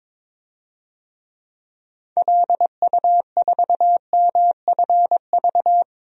Morse code, heard as a keyed tone, about 22 wpm, 700 Hz. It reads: LU4MFV
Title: DXCC-100 Call Signs - 6 Characters - 22wpm